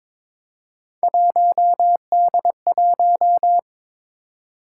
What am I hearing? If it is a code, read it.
Morse code, heard as a keyed tone, about 22 wpm, 700 Hz. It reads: 1D1
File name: practice / 22wpm / Three Letter-Number Combinations - 22wpm